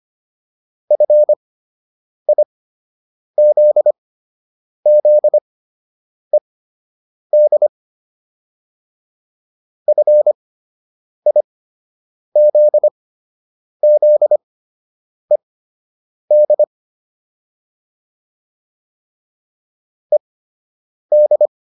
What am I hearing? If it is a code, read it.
Morse code, heard as a keyed tone, about 25 wpm, spaced as for 8 wpm, 600 Hz. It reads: FIZZED FIZZED ED
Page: fundamental-practice-files